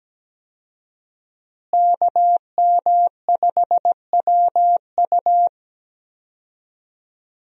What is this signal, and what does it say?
Morse code, heard as a keyed tone, about 17 wpm, 700 Hz. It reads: KM5WU
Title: USA Call Signs - 5 Characters - Repeated 2x - 17wpm